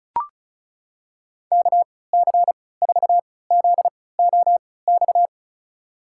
Morse code, heard as a keyed tone, about 34 wpm, spaced as for 20 wpm, 700 Hz. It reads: KC4ZOX
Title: USA Call Signs - 6 Characters - 3x Character Spacing - 35wpm